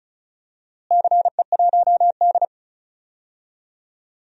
Morse code, heard as a keyed tone, about 36 wpm, 700 Hz. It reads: CE1D
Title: DXCC-100 Call Signs - 4 Characters - Repeated 2x - 35wpm